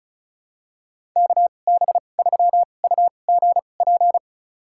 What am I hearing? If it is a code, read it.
Morse code, heard as a keyed tone, about 36 wpm, spaced as for 25 wpm, 700 Hz. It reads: KB3UGP
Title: USA Call Signs - 2x Character Spacing - 35wpm